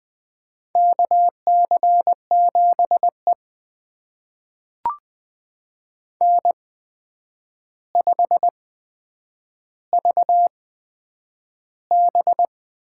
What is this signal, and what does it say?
Morse code, heard as a keyed tone, about 20 wpm, spaced as for 9 wpm, 700 Hz. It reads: KC7E N5VB
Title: USA Call Signs - 4 Characters - 8x Character Spacing - 20wpm